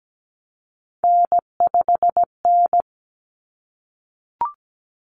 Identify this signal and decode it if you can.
Morse code, heard as a keyed tone, about 17 wpm, 700 Hz. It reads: N5N
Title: Common Cut Numbers - 3-Digit 17wpm